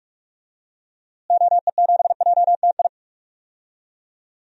Morse code, heard as a keyed tone, about 46 wpm, 700 Hz. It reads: OE7JTI
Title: DXCC-100 Call Signs - 6 Characters - Repeated 3x - 45wpm